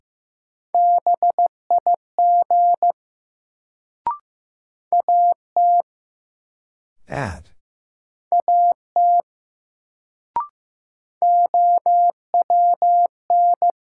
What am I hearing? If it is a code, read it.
Morse code, heard as a keyed tone, about 15 wpm, 700 Hz. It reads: BIG AT AT OWN
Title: Sets of 2 Words - New Word Drill - 15wpm